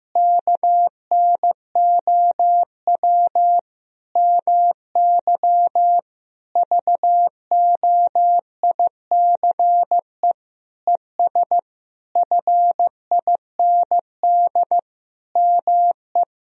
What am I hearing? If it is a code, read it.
Morse code, heard as a keyed tone, about 16 wpm, 700 Hz. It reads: KNOW MY VOICE ES FIND ME
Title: Sentences from Top 500 Words and Abbreviations 15wpm